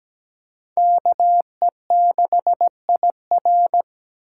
Morse code, heard as a keyed tone, about 17 wpm, 700 Hz. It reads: KE6IR
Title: USA Call Signs - 5 Characters - 2x Character Spacing - 17wpm